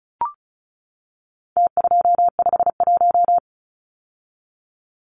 Morse code, heard as a keyed tone, about 34 wpm, 700 Hz. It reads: T251
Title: Numbers - Contest Style - 4-Digit 35wpm